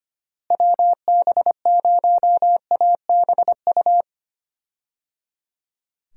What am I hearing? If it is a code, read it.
Morse code, heard as a keyed tone, about 25 wpm, 700 Hz. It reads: WB0ABU
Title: USA Call Signs - 6 Characters - Repeated 3x - 25wpm